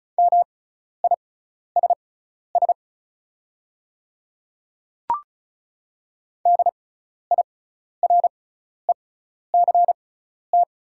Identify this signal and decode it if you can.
Morse code, heard as a keyed tone, about 34 wpm, spaced as for 12 wpm, 700 Hz. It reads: MISS DIRECT
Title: 6x Character Spacing - Top 500 Words - 35wpm